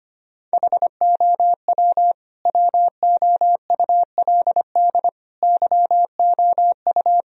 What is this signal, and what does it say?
Morse code, heard as a keyed tone, about 25 wpm, 700 Hz. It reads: HOW WOULD YOU
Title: Sentences from Top 1000 Words - Review Part 1 - 25wpm